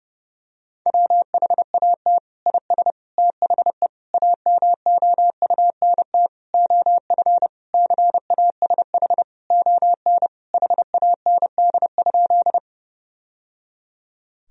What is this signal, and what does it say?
Morse code, heard as a keyed tone, about 31 wpm, 700 Hz. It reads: WHAT IS THE AMOUNT OF CASH ON HAND?
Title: Sentences from Top 1000 Words - Review Part 4 - 30wpm